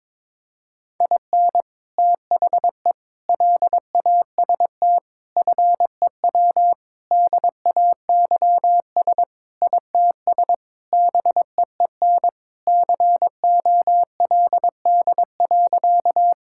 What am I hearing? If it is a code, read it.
Morse code, heard as a keyed tone, about 22 wpm, 700 Hz. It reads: IN THE LAST FEW DAYS ITS BEEN COLD.
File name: practice / 22wpm / Sentences with 3-Word Phrases - 22wpm